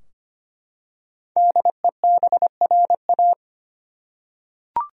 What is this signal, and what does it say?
Morse code, heard as a keyed tone, about 25 wpm, 700 Hz. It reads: DEBRA
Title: USA Common Names - Max 5 Letters - 25wpm